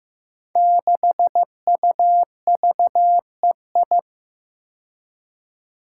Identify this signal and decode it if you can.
Morse code, heard as a keyed tone, about 15 wpm, 700 Hz. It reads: 6UVEI
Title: Five Challenging Letter-Number Combinations 15wpm